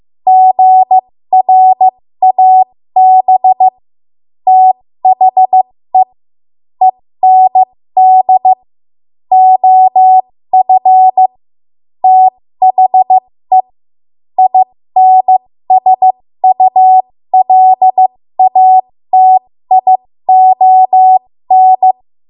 Morse code, heard as a keyed tone, about 15 wpm, spaced as for 13 wpm, 750 Hz. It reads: GRAB THE END OF THE INSULATION
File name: Morse / 13WPM